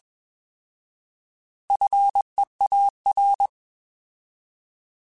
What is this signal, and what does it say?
Morse code, heard as a keyed tone, about 21 wpm, 800 Hz. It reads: FEAR
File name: Alphabetical / Alphabetical mp3s / Fs @20